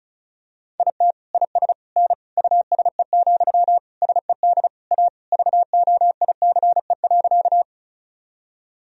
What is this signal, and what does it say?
Morse code, heard as a keyed tone, about 36 wpm, 700 Hz. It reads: IT IS N USE, SED A VOICE.
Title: Sentences from Top 500 Words and Abbreviations 35wpm